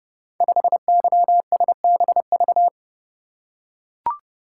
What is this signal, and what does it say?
Morse code, heard as a keyed tone, about 31 wpm, 700 Hz. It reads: 5YSBV
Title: Code Groups - 5 Letter:Numbers - 30wpm